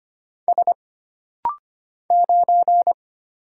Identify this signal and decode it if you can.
Morse code, heard as a keyed tone, about 25 wpm, 700 Hz. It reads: S 9
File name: Single Character (Letter:Number:Punctuation:Prosign) - Rapid-Fire - 25wpm